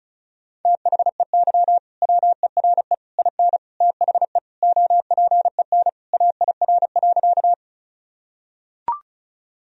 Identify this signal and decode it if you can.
Morse code, heard as a keyed tone, about 36 wpm, 700 Hz. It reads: THEY WERE IN THE OPEN AIR.
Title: Sentences from Top 1000 Words - Review Part 1 - 35wpm